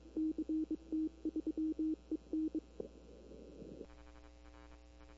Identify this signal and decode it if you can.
Morse code, heard as a keyed tone, about 24 wpm, 310 Hz. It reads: CT3EN
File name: CW / 2005-k5tr arrl cw dx -20:00